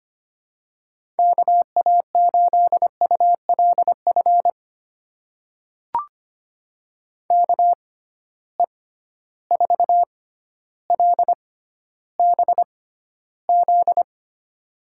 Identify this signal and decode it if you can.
Morse code, heard as a keyed tone, about 25 wpm, spaced as for 13 wpm, 700 Hz. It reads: KA8ULF KE4LBZ
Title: USA Call Signs - 6 Characters - 6x Character Spacing - 25wpm